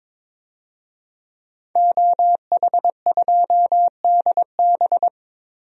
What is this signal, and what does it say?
Morse code, heard as a keyed tone, about 21 wpm, 700 Hz. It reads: OH2DB
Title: DXCC-100 Call Signs - 5 Characters - Repeated 2x - 22wpm